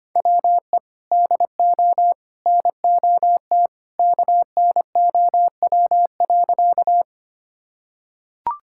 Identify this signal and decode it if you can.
Morse code, heard as a keyed tone, about 25 wpm, 700 Hz. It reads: WE DO NOT KNOW.
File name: practice / 25wpm / Sentences with 3-Word Phrases - 25wpm